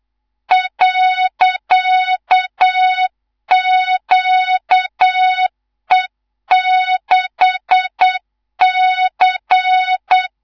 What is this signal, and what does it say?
Morse code, heard as a keyed tone, about 8 wpm, 3700 Hz, with noise. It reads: .QE6C
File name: Morze-HA2MN / hang / 19 GYAKORLAT